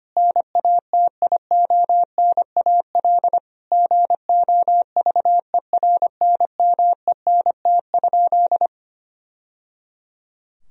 Morse code, heard as a keyed tone, about 25 wpm, 700 Hz. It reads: NATIONAL GOVERNMENT?
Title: Sentences from Top 1000 Words - Review Part 4 - 25wpm